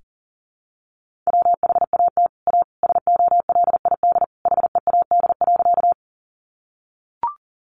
Morse code, heard as a keyed tone, about 40 wpm, 700 Hz. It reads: WHAT A SOLID HEAD.
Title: Sentences from Top 1500 Words 40wpm